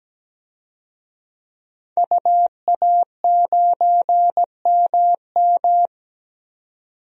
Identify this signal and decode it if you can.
Morse code, heard as a keyed tone, about 17 wpm, 700 Hz. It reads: UA9MM
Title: DXCC-100 Call Signs - 5 Characters - 17wpm